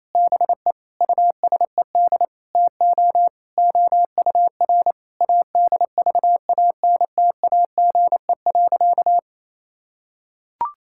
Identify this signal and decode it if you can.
Morse code, heard as a keyed tone, about 28 wpm, 700 Hz. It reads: BE USED TO OUR ADVANTAGE.
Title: Sentences from Top 1000 Words - Review Part 6 - 28wpm